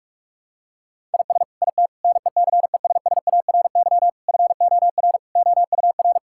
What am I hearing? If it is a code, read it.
Morse code, heard as a keyed tone, about 41 wpm, 700 Hz. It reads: IS IT NECESSARY FOR OUR
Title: Sentences from Top 1000 Words - Review Part 6 - 45wpm